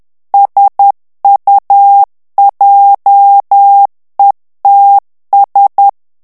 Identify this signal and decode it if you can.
Morse code, heard as a keyed tone, about 10 wpm, 800 Hz. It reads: SUJETS
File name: FRANCAIS TEXTE CLAIR 2 VITESSE 10 MOTS MINUTES